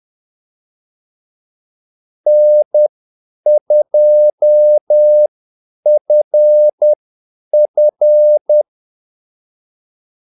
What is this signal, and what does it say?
Morse code, heard as a keyed tone, about 10 wpm, spaced as for 8 wpm, 600 Hz. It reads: N2FF
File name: Callsigns-10-8wpm